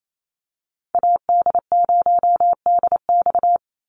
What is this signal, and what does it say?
Morse code, heard as a keyed tone, about 28 wpm, 700 Hz. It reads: AD0DX
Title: Call Signs Repeated 3x - 28wpm